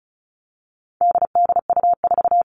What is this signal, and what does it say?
Morse code, heard as a keyed tone, about 34 wpm, 700 Hz. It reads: DDU4
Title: All Cut Numbers - 4-Digit 35wpm